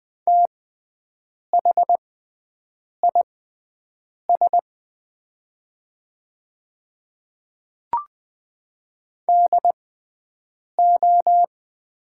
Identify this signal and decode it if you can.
Morse code, heard as a keyed tone, about 20 wpm, spaced as for 7 wpm, 690 Hz. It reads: THIS DO
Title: 6x Character Spacing - Top 100 Words - 20wpm